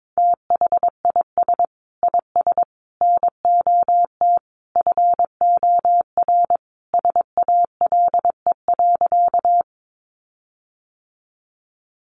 Morse code, heard as a keyed tone, about 22 wpm, 700 Hz. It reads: THIS IS NOT FOR SALE.
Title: Sentences from Top 1400 Words 22wpm